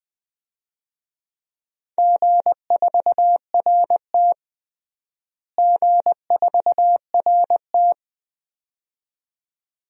Morse code, heard as a keyed tone, about 20 wpm, 700 Hz. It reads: G4RT G4RT
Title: DXCC-100 Call Signs - 4 Characters - Repeated 2x - 20wpm